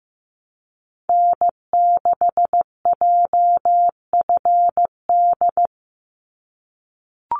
Morse code, heard as a keyed tone, about 15 wpm, 700 Hz. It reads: N6JFD
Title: NA SOTA Activator Callsigns 15wpm